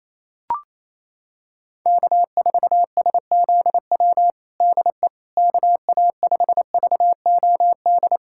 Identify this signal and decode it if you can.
Morse code, heard as a keyed tone, about 28 wpm, 700 Hz. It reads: K4SZW DE KA5VOD
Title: USA Callsign DE Callsign 28wpm